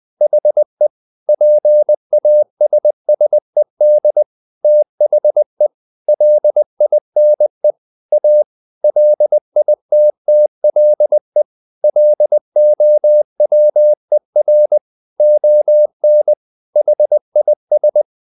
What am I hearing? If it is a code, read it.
Morse code, heard as a keyed tone, about 20 wpm, 600 Hz. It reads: HE PASSED THE LINE A LITTLE LOWER ON HIS
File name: the old man and the sea part 2 20wpm